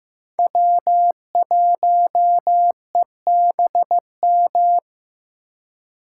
Morse code, heard as a keyed tone, about 15 wpm, 700 Hz. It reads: W1EBM
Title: K1USN SST QSO Practice - 15wpm